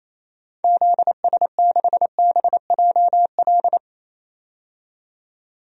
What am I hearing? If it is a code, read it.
Morse code, heard as a keyed tone, about 28 wpm, 700 Hz. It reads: ZS6BJL
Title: DXCC-100 Call Signs - 6 Characters - 28wpm